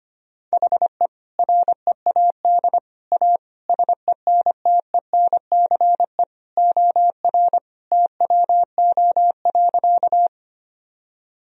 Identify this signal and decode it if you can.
Morse code, heard as a keyed tone, about 25 wpm, 700 Hz. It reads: HE READ A SENTENCE OR TWO.
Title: Sentences from Top 1000 Words - Review Part 1 - 25wpm